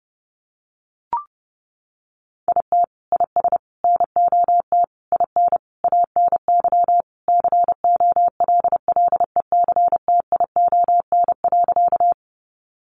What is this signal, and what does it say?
Morse code, heard as a keyed tone, about 30 wpm, 700 Hz. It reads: IT IS NOT IN ANY COLLECTION.
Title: Sentences from Top 1200 Words 30wpm